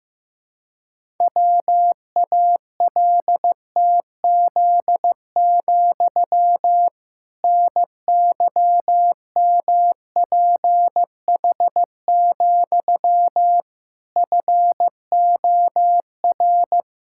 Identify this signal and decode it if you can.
Morse code, heard as a keyed tone, about 15 wpm, 700 Hz. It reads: WALTZ, NYMPH, FOR
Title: Pangram Challenge v2 - 15wpm